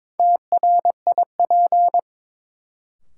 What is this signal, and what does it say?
Morse code, heard as a keyed tone, about 22 wpm, 700 Hz. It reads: TRIP
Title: Sentences from Top 1000 Words - Review Part 4 - New Word Drill - 22wpm